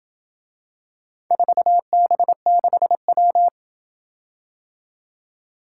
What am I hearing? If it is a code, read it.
Morse code, heard as a keyed tone, about 28 wpm, 700 Hz. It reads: 4B6W